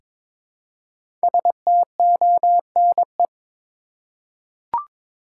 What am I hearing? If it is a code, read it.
Morse code, heard as a keyed tone, about 22 wpm, 700 Hz. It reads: STONE
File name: 5 Letter Words - First 3 Letters Given - 22wpm